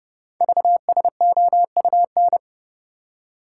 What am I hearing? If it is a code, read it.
Morse code, heard as a keyed tone, about 28 wpm, 690 Hz. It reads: VSOUN
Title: Code Groups - 5 Letters - 30wpm